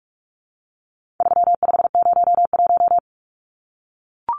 Morse code, heard as a keyed tone, about 44 wpm, 700 Hz. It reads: 3501
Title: Numbers - 4-Digit 45wpm